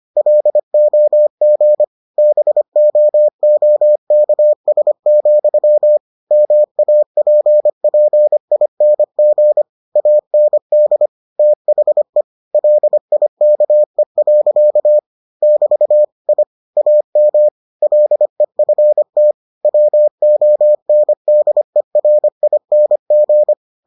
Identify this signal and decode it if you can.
Morse code, heard as a keyed tone, about 25 wpm, 600 Hz. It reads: LOG BOOKS, MAPPING AND THE LIKE. <BT> I AM LEFT WONDERING